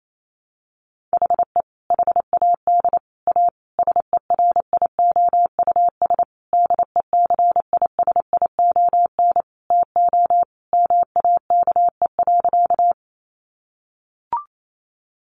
Morse code, heard as a keyed tone, about 28 wpm, 700 Hz. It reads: HE HAD A SERIOUS DECISION TO MAKE.
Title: Sentences from Top 1100 Words 28wpm